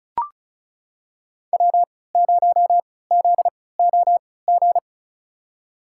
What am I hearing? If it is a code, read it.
Morse code, heard as a keyed tone, about 36 wpm, spaced as for 20 wpm, 700 Hz. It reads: W0ZOG
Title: USA Call Signs - 5 Characters - 3x Character Spacing - 35wpm